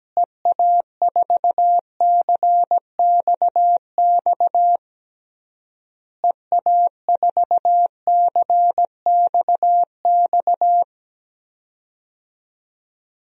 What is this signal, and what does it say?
Morse code, heard as a keyed tone, about 17 wpm, 700 Hz. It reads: EA4CXX EA4CXX
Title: DXCC-100 Call Signs - 6 Characters - Repeated 3x - 17wpm